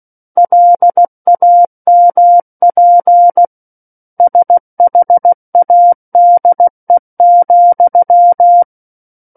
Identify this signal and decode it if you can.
Morse code, heard as a keyed tone, about 16 wpm, 700 Hz. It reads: LAMP SHADE,
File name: Book One / 07-Chapter 7 - 16 WPM